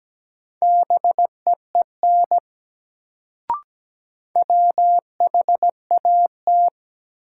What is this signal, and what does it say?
Morse code, heard as a keyed tone, about 17 wpm, 700 Hz. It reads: BEEN WHAT
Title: Sentences from Top 100 Words - New Word Drill - 17wpm